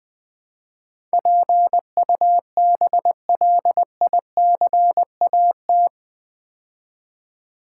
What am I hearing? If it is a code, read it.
Morse code, heard as a keyed tone, about 20 wpm, 700 Hz. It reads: PUBLICAT
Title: Build A Word - 4 Letters At A Time - 11-Letter Words - 20wpm